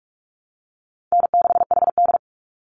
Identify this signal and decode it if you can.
Morse code, heard as a keyed tone, about 44 wpm, 700 Hz. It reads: N6HD
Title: Call Signs Repeated 2x - 45wpm